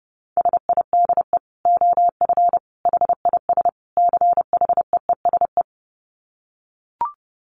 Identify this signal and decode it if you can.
Morse code, heard as a keyed tone, about 30 wpm, 700 Hz. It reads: SIDE OF HIS CHEESE
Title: Dit Challenge 30wpm